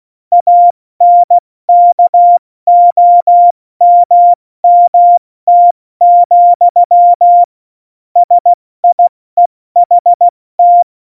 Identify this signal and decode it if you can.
Morse code, heard as a keyed tone, about 16 wpm, spaced as for 14 wpm, 700 Hz. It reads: ANKOMMT, SIEHT
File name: Morsekurs / Witze / Witz 17. November 2020, 70 F80